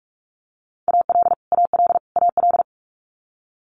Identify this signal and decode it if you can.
Morse code, heard as a keyed tone, about 46 wpm, 700 Hz. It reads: AL AL AL
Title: Multi-letter Phonemes Repeated 3x 45wpm